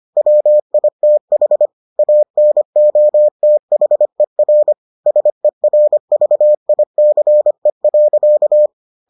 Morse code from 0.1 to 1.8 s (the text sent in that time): WITH